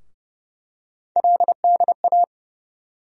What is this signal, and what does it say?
Morse code, heard as a keyed tone, about 29 wpm, 700 Hz. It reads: LDA
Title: USA Call Sign Postfixes - 3 Letters 30wpm